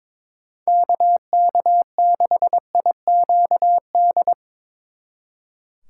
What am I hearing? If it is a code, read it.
Morse code, heard as a keyed tone, about 25 wpm, 700 Hz. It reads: KK6IQD
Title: Speed Racing - USA 2x3 Call Signs 22wpm